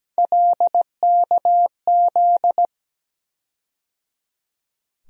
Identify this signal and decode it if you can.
Morse code, heard as a keyed tone, about 17 wpm, 700 Hz. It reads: LKZ
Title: USA Call Sign Postfixes - 3 Letters 17wpm